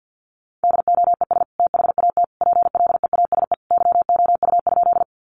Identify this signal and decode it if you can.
Morse code, heard as a keyed tone, about 50 wpm, 700 Hz. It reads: DOES THAT PLEASE YOU?
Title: Sentences from Top 1500 Words 50wpm